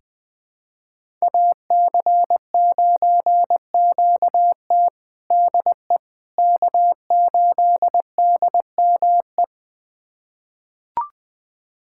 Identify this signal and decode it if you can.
Morse code, heard as a keyed tone, about 20 wpm, 700 Hz. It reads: AC9QT DE K8DME
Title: USA Callsign DE Callsign - 5 Characters - 20wpm